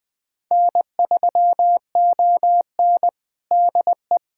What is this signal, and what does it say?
Morse code, heard as a keyed tone, about 20 wpm, 700 Hz. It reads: N3ON DE
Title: USA Callsign DE Callsign - 4 Characters - 20wpm